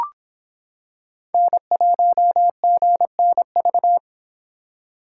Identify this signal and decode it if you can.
Morse code, heard as a keyed tone, about 26 wpm, 700 Hz. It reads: N1GNV